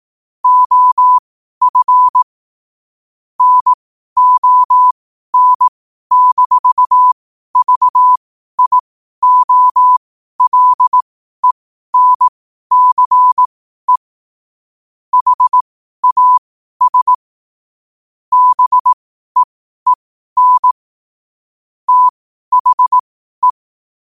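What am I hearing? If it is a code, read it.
Morse code, heard as a keyed tone, about 18 wpm, spaced as for 12 wpm, 1000 Hz. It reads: OF NON-VIOLENCE HAS BEEN THE
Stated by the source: Morse code quotes at 12 words per minute, by ai4qr.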